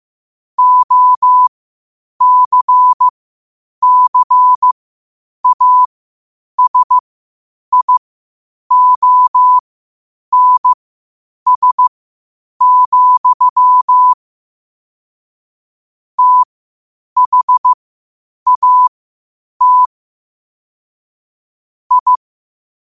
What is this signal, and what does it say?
Morse code, heard as a keyed tone, about 15 wpm, spaced as for 8 wpm, 1000 Hz. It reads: OCCASIONS, THAT I
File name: qotdfiles / 8wpm